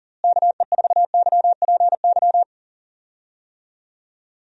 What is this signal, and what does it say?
Morse code, heard as a keyed tone, about 37 wpm, 700 Hz. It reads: KE4YPY
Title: USA Call Signs - 6 Characters - Repeated 3x - 40wpm